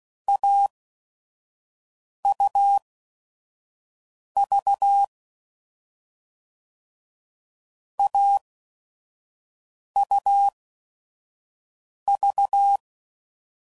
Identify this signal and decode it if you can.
Morse code, heard as a keyed tone, about 16 wpm, spaced as for 5 wpm, 800 Hz. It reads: AUV AUV
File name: Combinations / Difficult Combos / AUV AUV4